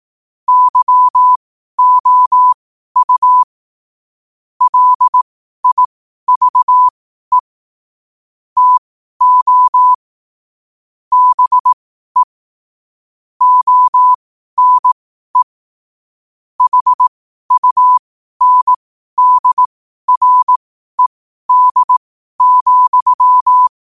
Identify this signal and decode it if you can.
Morse code, heard as a keyed tone, about 18 wpm, spaced as for 12 wpm, 1000 Hz. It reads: YOU LIVE TO BE ONE HUNDRED,
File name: qotdfiles / 12wpm